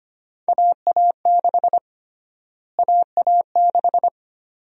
Morse code, heard as a keyed tone, about 25 wpm, 700 Hz. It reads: AA6 AA6
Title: USA Call Sign Prefixes - Repeated 2x - 25wpm